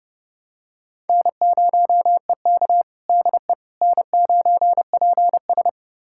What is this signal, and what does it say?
Morse code, heard as a keyed tone, about 32 wpm, 700 Hz. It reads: N0EK DE N9PS
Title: USA Callsign DE Callsign - 4 Characters - 30wpm